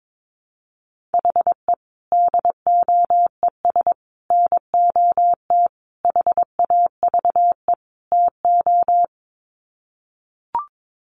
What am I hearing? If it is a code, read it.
Morse code, heard as a keyed tone, about 22 wpm, 700 Hz. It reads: HE DOES NOT HAVE TO
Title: Sets of 5 Words Encore 22wpm